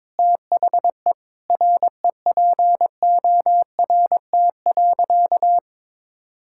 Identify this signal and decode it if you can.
Morse code, heard as a keyed tone, about 22 wpm, 700 Hz. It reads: THE REPORT.
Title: Sentences from Top 1000 Words - Review Part 6 - 22wpm